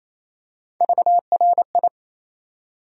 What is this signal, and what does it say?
Morse code, heard as a keyed tone, about 28 wpm, 700 Hz. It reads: VRI
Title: USA Call Sign Postfixes - 3 Letters 28wpm